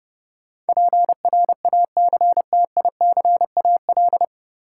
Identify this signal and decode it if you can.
Morse code, heard as a keyed tone, about 32 wpm, 700 Hz. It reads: PRACTICAL
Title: Sentences from Top 1400 Words - New Word Drill Alt - 30wpm